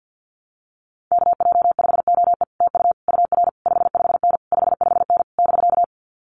Morse code, heard as a keyed tone, about 50 wpm, 700 Hz. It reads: KJ5OE TU UR 55N 55N <BK>
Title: Common QSO Phrases Encore 50wpm